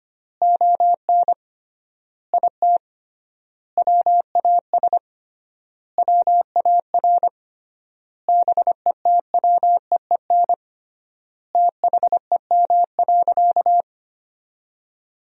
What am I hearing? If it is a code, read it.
Morse code, heard as a keyed tone, about 25 wpm, 700 Hz. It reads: ON IT WAS WAR BETWEEN THEM.
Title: 3x Word-Spacing - Sentences from Top 400 Words - 25wpm